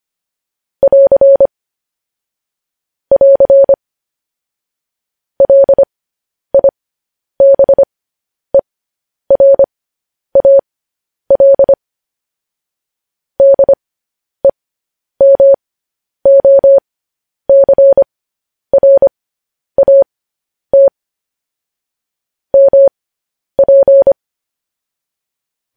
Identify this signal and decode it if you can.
Morse code, heard as a keyed tone, about 25 wpm, spaced as for 10 wpm, 550 Hz. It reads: <AR> <AR> LIBERAL DEMOCRAT MP
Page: news headlines in morse code at 10 wpm